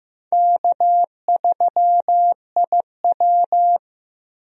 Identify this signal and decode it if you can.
Morse code, heard as a keyed tone, about 16 wpm, 700 Hz. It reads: K3IW
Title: USA Call Signs - 4 Characters - Repeated 3x - 15wpm